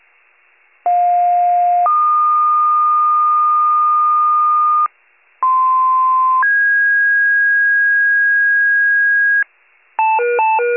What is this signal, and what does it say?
Two-tone page, heard as a one-second tone, then a three-second tone, about 710 then 1220 Hz, about 1040 then 1650 Hz, with noise. It is heard from 0.9 to 9.4 s.